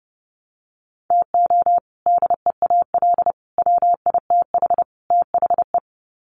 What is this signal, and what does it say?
Morse code, heard as a keyed tone, about 30 wpm, 700 Hz. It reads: TO DEAL WITH THE
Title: Sets of 4 Words 30wpm